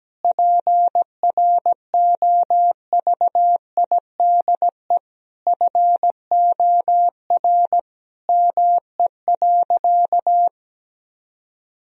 Morse code, heard as a keyed tone, about 17 wpm, 700 Hz. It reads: PROVIDE FOR ME.
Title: Sentences from Top 1000 Words - Review Part 3 - 17wpm